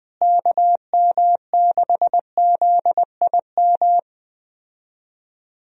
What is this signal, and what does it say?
Morse code, heard as a keyed tone, about 20 wpm, 700 Hz. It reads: KM6ZIM
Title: USA Call Signs - 6 Characters - Repeated 2x - 20wpm